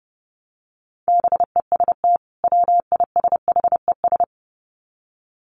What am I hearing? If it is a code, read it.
Morse code, heard as a keyed tone, about 30 wpm, 700 Hz. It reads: BEST WISHES
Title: Common QSO Phrases 30wpm